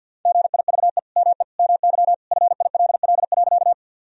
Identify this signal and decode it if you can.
Morse code, heard as a keyed tone, about 50 wpm, 700 Hz. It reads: GIVE ME MY FILL.
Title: Sentences from Top 1000 Words - Review Part 3 - 50wpm